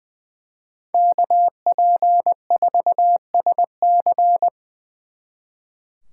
Morse code, heard as a keyed tone, about 23 wpm, 700 Hz. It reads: KP4SC
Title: Speed Racing - USA 2x2 Call Signs 20wpm